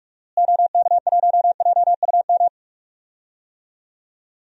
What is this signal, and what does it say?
Morse code, heard as a keyed tone, about 46 wpm, 700 Hz. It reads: OK1JUM
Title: DXCC-100 Call Signs - 6 Characters - Repeated 2x - 45wpm